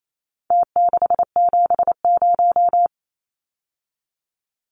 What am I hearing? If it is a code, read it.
Morse code, heard as a keyed tone, about 28 wpm, 700 Hz. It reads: T670
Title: Numbers - Contest Style - 4-Digit 28wpm